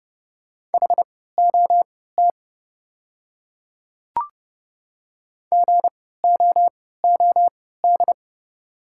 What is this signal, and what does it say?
Morse code, heard as a keyed tone, about 30 wpm, spaced as for 17 wpm, 690 Hz. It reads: HOT GOOD
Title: 3x Character Spacing - Top 500 Words - 30wpm